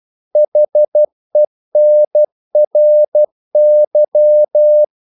Morse code, heard as a keyed tone, about 12 wpm, 600 Hz. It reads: HENRY
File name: CW / Practice / EnglishQuotations0312